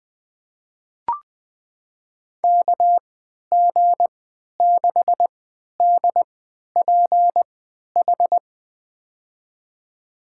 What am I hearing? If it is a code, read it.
Morse code, heard as a keyed tone, about 20 wpm, spaced as for 11 wpm, 700 Hz. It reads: KG6DPH
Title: USA Call Signs - 3x Character Spacing - 20wpm